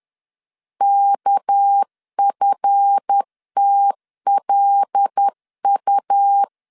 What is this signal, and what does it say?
Morse code, heard as a keyed tone, about 10 wpm, 800 Hz. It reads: KFTLU
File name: LEZIONI / 9 2 10wpm 2736